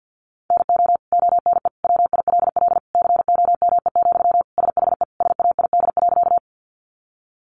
Morse code, heard as a keyed tone, about 50 wpm, 700 Hz. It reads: NO ONE WILL COME, SHE SAID.
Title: Sentences from Top 100 Words 50wpm